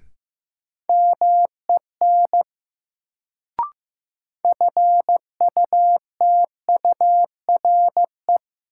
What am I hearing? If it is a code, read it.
Morse code, heard as a keyed tone, about 15 wpm, 700 Hz. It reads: MEN FUTURE
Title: Sets of 3 Words - New Word Drill - 15wpm